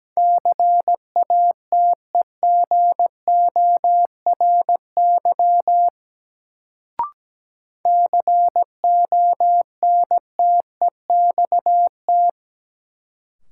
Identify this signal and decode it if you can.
Morse code, heard as a keyed tone, about 17 wpm, 700 Hz. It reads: CATEGORY CONTEXT
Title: Sentences from Top 1000 Words - Review Part 7 - New Word Drill - 17wpm